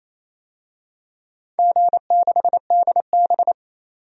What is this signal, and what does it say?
Morse code, heard as a keyed tone, about 27 wpm, 690 Hz. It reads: G6DB
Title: DXCC-100 Call Signs - 4 Characters - 28wpm